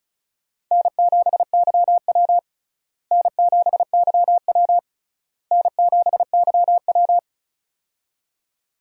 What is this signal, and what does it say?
Morse code, heard as a keyed tone, about 36 wpm, 700 Hz. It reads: N7YW N7YW N7YW
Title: USA Call Signs - 4 Characters - Repeated 3x - 35wpm